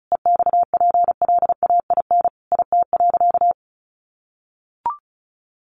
Morse code, heard as a keyed tone, about 34 wpm, 700 Hz. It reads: EXPLAIN IT.
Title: Sentences from Top 1100 Words 35wpm